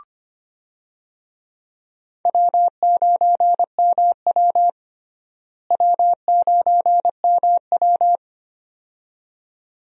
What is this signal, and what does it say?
Morse code, heard as a keyed tone, about 25 wpm, 700 Hz. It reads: W9MW W9MW
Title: USA Call Signs - 4 Characters - Repeated 2x - 25wpm